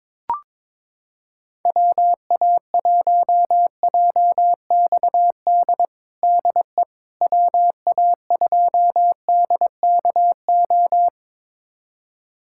Morse code, heard as a keyed tone, about 21 wpm, 700 Hz. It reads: WA1JXD DE WA2DKO
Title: USA Callsign DE Callsign - 6 Characters - 22wpm